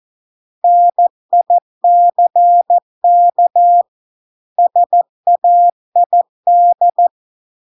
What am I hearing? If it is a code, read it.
Morse code, heard as a keyed tone, about 14 wpm, 700 Hz. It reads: NICK SAID